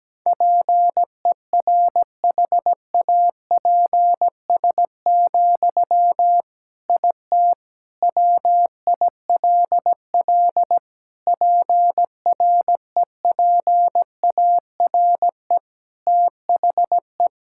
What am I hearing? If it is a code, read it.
Morse code, heard as a keyed tone, about 17 wpm, 700 Hz. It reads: PERHAPS, IT WILL PREPARE THE
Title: Sentences from Top 1000 Words - Review Part 6 - 17wpm